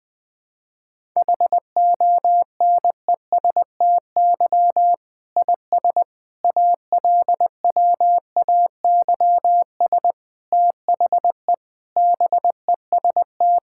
Morse code, heard as a keyed tone, about 20 wpm, 700 Hz. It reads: HONESTY IS ALWAYS THE BEST
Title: Difficult Sentences with Hints - 20wpm